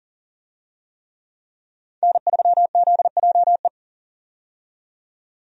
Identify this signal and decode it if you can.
Morse code, heard as a keyed tone, about 40 wpm, 700 Hz. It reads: N3ZJE
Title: USA Call Signs - 5 Characters - 40wpm